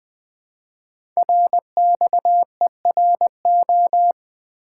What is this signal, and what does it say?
Morse code, heard as a keyed tone, about 21 wpm, 700 Hz. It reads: RXERO
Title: Code Groups - 5 Letters - 20wpm